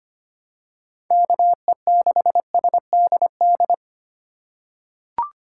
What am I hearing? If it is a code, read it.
Morse code, heard as a keyed tone, about 25 wpm, 700 Hz. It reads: KE6SDD
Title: USA Call Signs - 6 Characters - 4x Character Spacing - 25wpm